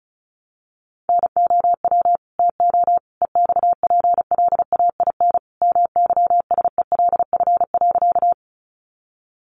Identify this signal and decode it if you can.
Morse code, heard as a keyed tone, about 34 wpm, 700 Hz. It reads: NOW TO EXPLAIN MYSELF.
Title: Sentences from Top 1300 Words 35wpm